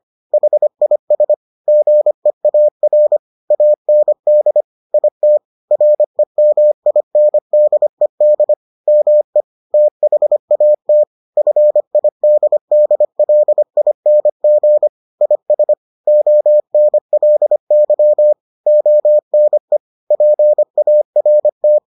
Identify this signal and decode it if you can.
Morse code, heard as a keyed tone, about 25 wpm, 600 Hz. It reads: HIS GEAR AND IT REMINDED ME THAT FIDDLING IS ONLY ONE PART